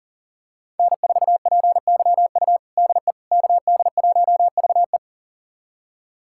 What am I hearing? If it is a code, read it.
Morse code, heard as a keyed tone, about 40 wpm, 700 Hz. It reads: N4PYU DE KD1VE
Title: USA Callsign DE Callsign - 5 Characters - 40wpm